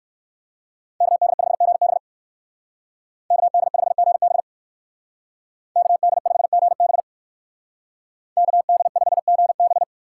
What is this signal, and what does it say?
Morse code, heard as a keyed tone, about 63 wpm, 700 Hz. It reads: KD5GB KD5GB KD5GB KD5GB